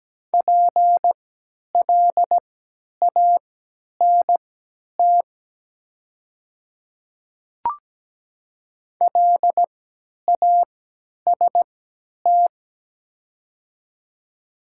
Morse code, heard as a keyed tone, about 17 wpm, spaced as for 10 wpm, 690 Hz. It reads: PLANT LAST
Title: 3x Character Spacing - Top 500 Words - 17wpm